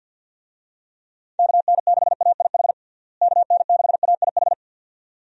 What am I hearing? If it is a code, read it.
Morse code, heard as a keyed tone, about 51 wpm, 700 Hz. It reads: KN6AIH KN6AIH
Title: USA Call Signs - 6 Characters - Repeated 2x - 50wpm